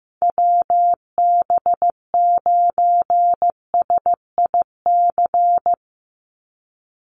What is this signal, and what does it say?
Morse code, heard as a keyed tone, about 15 wpm, 700 Hz. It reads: WB9SIC
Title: USA Call Signs 15wpm